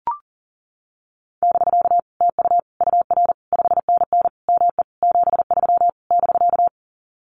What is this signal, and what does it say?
Morse code, heard as a keyed tone, about 40 wpm, 700 Hz. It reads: <BK> TU UR 5NN ME 73 <BK>
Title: Common QSO Phrases Encore 40wpm